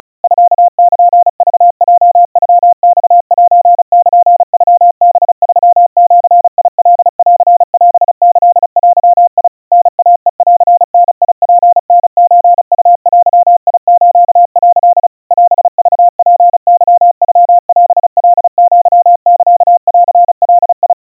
Morse code (Japanese Wabun code), heard as a keyed tone, about 35 wpm, 700 Hz. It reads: ミルクヲノマセルノハ3ジカンオキデ タイヘンダツタソウデス」 オクツケノオカアサンガ
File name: kawa / Morse sample / kamyu 130 cpm 700 Hz Crispy